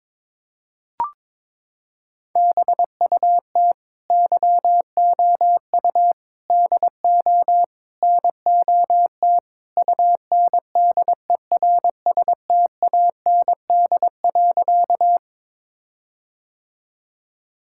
Morse code, heard as a keyed tone, about 22 wpm, 700 Hz. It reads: BUT YOU DO NOT UNDERSTAND.
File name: practice / 22wpm / Sentences with 4-Word Phrases - 22wpm